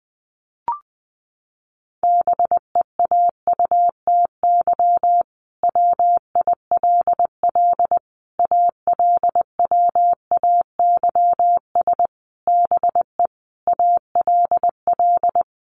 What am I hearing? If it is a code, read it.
Morse code, heard as a keyed tone, about 20 wpm, 700 Hz. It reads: BEAUTY WILL ALWAYS BE ALL
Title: Sentences from Top 1300 Words 20wpm